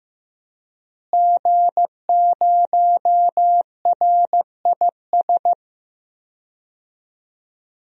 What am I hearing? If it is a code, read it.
Morse code, heard as a keyed tone, about 16 wpm, 700 Hz. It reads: G0RIS
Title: DXCC-100 Call Signs - 5 Characters - Repeated 3x - 15wpm